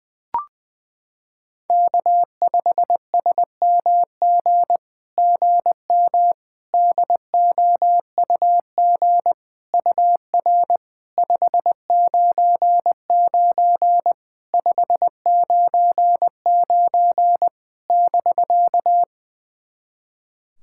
Morse code, heard as a keyed tone, about 21 wpm, 700 Hz. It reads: K5SMG GM DOUG UR 599 599 <BK>
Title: SOTA QSO Practice 20wpm